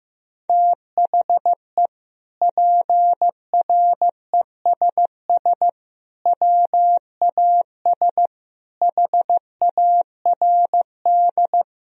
Morse code, heard as a keyed tone, about 16 wpm, 700 Hz. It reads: THE PRESS WAS HARD
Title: Sentences from Top 1000 Words - Review Part 1 - 15wpm